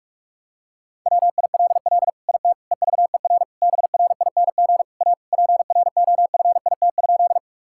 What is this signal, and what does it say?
Morse code, heard as a keyed tone, about 45 wpm, 700 Hz. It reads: WILL IT EVER BRING A PROFIT?
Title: Sentences from Top 1000 Words - Review Part 5 - 45wpm